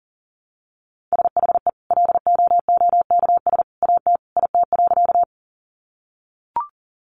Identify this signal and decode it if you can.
Morse code, heard as a keyed tone, about 40 wpm, 700 Hz. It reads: SHE LOOKS AT IT.
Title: Additional Easy Phrases 40wpm